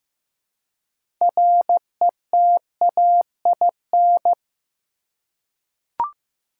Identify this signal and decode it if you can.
Morse code, heard as a keyed tone, about 15 wpm, 700 Hz. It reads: RETAIN
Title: Six Letter Words - 15wpm